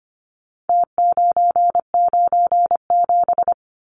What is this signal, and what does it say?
Morse code, heard as a keyed tone, about 25 wpm, 700 Hz. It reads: T997
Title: Numbers - Contest Style - 4-Digit 25wpm